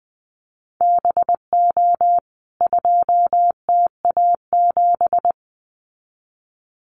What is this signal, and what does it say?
Morse code, heard as a keyed tone, about 20 wpm, 700 Hz. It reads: BO 2TA7
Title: CWT Exchange 20wpm